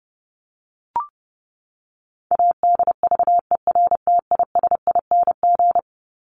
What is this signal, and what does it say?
Morse code, heard as a keyed tone, about 30 wpm, 700 Hz. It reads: ADVERTISING
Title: Top 751-1000 Words 30wpm